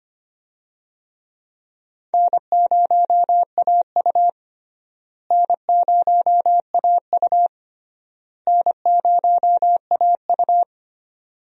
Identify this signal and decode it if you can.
Morse code, heard as a keyed tone, about 25 wpm, 700 Hz. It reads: N0AU N0AU N0AU
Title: USA Call Signs - 4 Characters - Repeated 3x - 25wpm